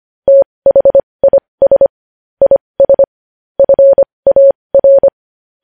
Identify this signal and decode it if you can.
Morse code, heard as a keyed tone, about 25 wpm, spaced as for 20 wpm, 550 Hz. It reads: THIS IS FAR
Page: News Headlines in Morse Code at 20 WPM